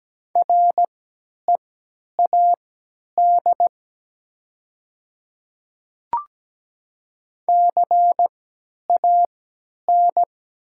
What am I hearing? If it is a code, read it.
Morse code, heard as a keyed tone, about 17 wpm, spaced as for 10 wpm, 700 Hz. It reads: READ CAN
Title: 3x Character Spacing - Top 200 Words - 17wpm